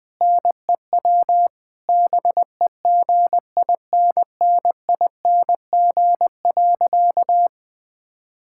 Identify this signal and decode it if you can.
Morse code, heard as a keyed tone, about 19 wpm, 700 Hz. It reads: NEW BEGINNING.
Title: Difficult Sentences with Hints - 20wpm